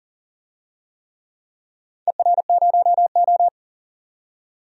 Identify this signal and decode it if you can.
Morse code, heard as a keyed tone, about 40 wpm, 700 Hz. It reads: ER0O
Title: DXCC-100 Call Signs - 4 Characters - 40wpm